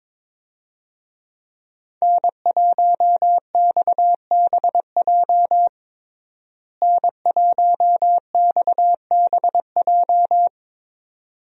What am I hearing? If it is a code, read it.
Morse code, heard as a keyed tone, about 22 wpm, 700 Hz. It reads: N1XBJ N1XBJ
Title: USA Call Signs - 5 Characters - Repeated 2x - 22wpm